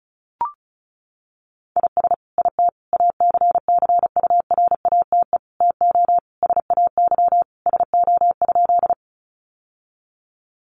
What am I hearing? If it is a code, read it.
Morse code, heard as a keyed tone, about 34 wpm, 700 Hz. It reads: IS IT ACCURATE TO SAY SO?
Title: Sentences from Top 1200 Words 35wpm